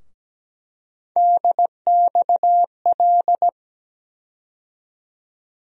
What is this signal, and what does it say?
Morse code, heard as a keyed tone, about 17 wpm, 700 Hz. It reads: DXL
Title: USA Call Sign Postfixes - 3 Letters 17wpm